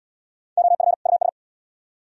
Morse code, heard as a keyed tone, about 71 wpm, 700 Hz. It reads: OF US